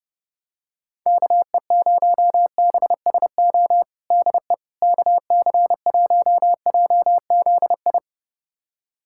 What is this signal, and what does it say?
Morse code, heard as a keyed tone, about 32 wpm, 700 Hz. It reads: KE0BSO DE KC1JZI
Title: USA Callsign DE Callsign - 6 Characters - 30wpm